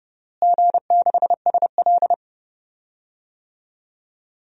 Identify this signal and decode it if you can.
Morse code, heard as a keyed tone, about 30 wpm, 700 Hz. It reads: G6SL
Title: DXCC-100 Call Signs - 4 Characters - Repeated 2x - 30wpm